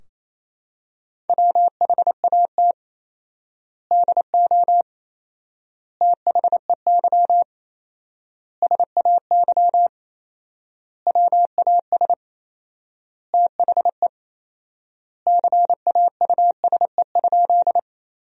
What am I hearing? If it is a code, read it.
Morse code, heard as a keyed tone, about 28 wpm, 700 Hz. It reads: WHAT DO THEY SAY WAS THE CAUSE?
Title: 4x Word-Spacing - Sentences from Top 200 Words - 28wpm